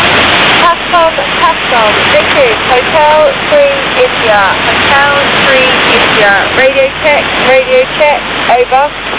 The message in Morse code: none

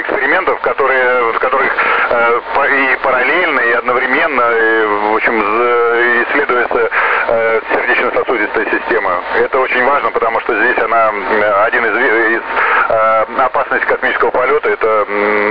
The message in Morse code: none

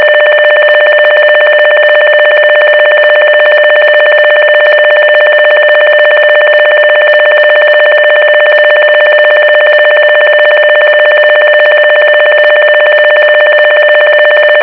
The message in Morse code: none

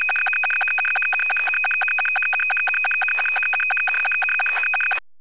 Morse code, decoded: none

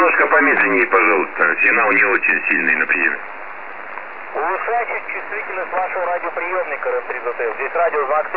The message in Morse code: none

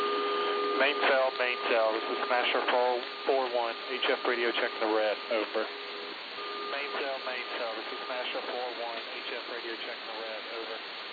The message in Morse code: none

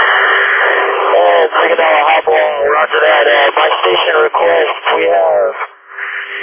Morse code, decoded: none

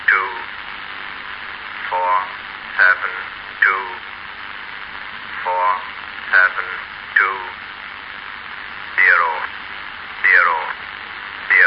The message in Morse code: none

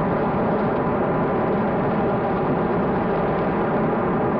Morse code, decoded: none

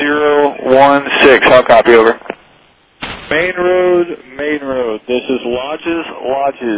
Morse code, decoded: none

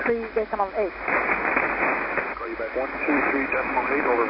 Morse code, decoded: none

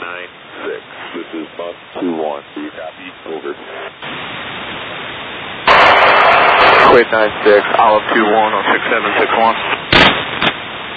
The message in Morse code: none

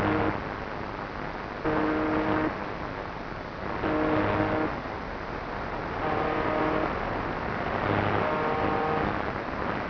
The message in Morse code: none

